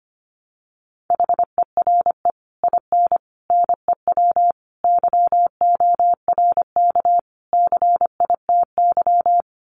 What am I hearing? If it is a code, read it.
Morse code, heard as a keyed tone, about 25 wpm, 700 Hz. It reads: HERE IN NEW YORK CITY
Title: Sets of 5 Words Encore2 25wpm